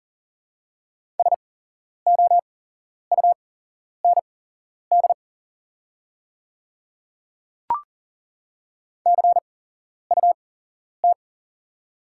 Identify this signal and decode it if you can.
Morse code, heard as a keyed tone, about 38 wpm, spaced as for 11 wpm, 700 Hz. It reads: SOUND CUT
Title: 8x Character Spacing - Top 300 Words - 40wpm